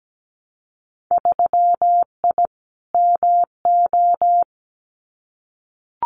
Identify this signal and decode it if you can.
Morse code, heard as a keyed tone, about 17 wpm, 700 Hz. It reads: 3I MO
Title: Winter Field Day Exchanges 17wpm